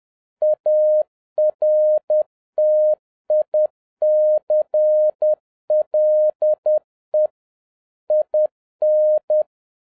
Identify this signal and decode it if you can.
Morse code, heard as a keyed tone, about 10 wpm, 610 Hz. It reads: ARTICLE IN